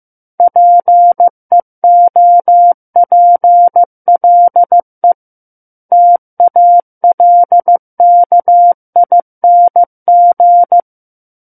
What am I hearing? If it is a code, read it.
Morse code, heard as a keyed tone, about 15 wpm, 700 Hz. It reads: PEOPLE TALKING
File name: Book One / 06-Chapter 6 - 15 WPM